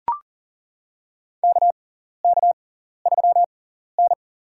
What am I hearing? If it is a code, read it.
Morse code, heard as a keyed tone, about 40 wpm, spaced as for 14 wpm, 700 Hz. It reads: KK3N
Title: USA Call Signs - 4 Characters - 6x Character Spacing - 40wpm